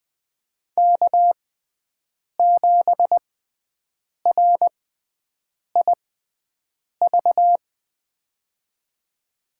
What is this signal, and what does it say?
Morse code, heard as a keyed tone, about 20 wpm, spaced as for 7 wpm, 700 Hz. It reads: K7RIV
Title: USA Call Signs - 5 Characters - 6x Character Spacing - 20wpm